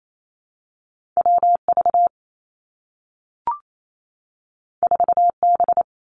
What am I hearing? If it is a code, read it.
Morse code, heard as a keyed tone, about 28 wpm, 700 Hz. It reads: W V 4 B
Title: Two Challenging Letter-Number Combinations 28wpm